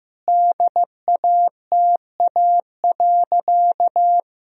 Morse code, heard as a keyed tone, about 16 wpm, 700 Hz. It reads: DATA.
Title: Sentences from Top 1000 Words - Review Part 3 - 15wpm